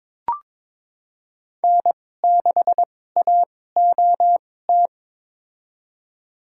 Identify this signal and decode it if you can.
Morse code, heard as a keyed tone, about 22 wpm, spaced as for 16 wpm, 700 Hz. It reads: N6AOT
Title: USA Call Signs - 2x Character Spacing - 22wpm